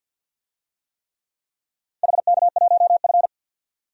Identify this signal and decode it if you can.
Morse code, heard as a keyed tone, about 52 wpm, 700 Hz. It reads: HK1F
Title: DXCC-100 Call Signs - 4 Characters - 50wpm